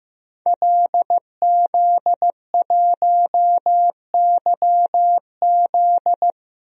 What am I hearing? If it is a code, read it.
Morse code, heard as a keyed tone, about 15 wpm, 700 Hz. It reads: LZ1YZ
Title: DXCC-100 Call Signs - 5 Characters - Repeated 2x - 15wpm